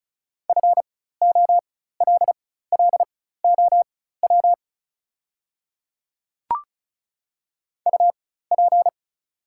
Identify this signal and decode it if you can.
Morse code, heard as a keyed tone, about 36 wpm, spaced as for 16 wpm, 700 Hz. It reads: FOLLOW UP
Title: 4x Character Spacing - Top 200 Words - 35wpm